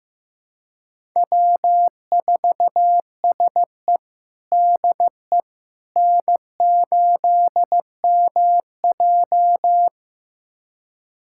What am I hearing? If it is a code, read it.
Morse code, heard as a keyed tone, about 15 wpm, 700 Hz. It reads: W4SE DE N8MJ
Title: USA Callsign DE Callsign - 4 Characters - 15wpm